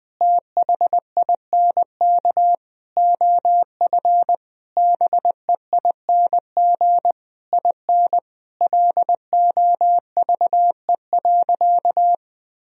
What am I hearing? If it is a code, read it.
Morse code, heard as a keyed tone, about 20 wpm, 700 Hz. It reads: THINK OF BEING IN LOVE.
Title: Sentences from Top 1000 Words - Review Part 3 - 20wpm